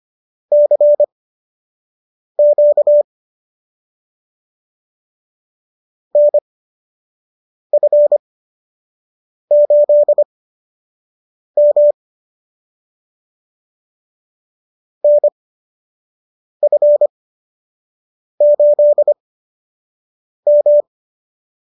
Morse code, heard as a keyed tone, about 25 wpm, spaced as for 6 wpm, 590 Hz. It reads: CQ NF8M NF8M